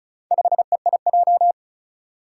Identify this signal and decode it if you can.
Morse code, heard as a keyed tone, about 36 wpm, 700 Hz. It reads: 5EIJ
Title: Four Challenging Letter-Number Combinations 35wpm